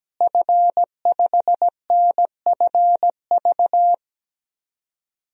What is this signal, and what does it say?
Morse code, heard as a keyed tone, about 17 wpm, 700 Hz. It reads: F5NFV
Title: DXCC-100 Call Signs - 5 Characters - Repeated 3x - 17wpm